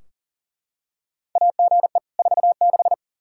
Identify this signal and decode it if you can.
Morse code, heard as a keyed tone, about 40 wpm, 700 Hz. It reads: AGE 46
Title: Practice QSO - Exchange 3 - More Info - 40wpm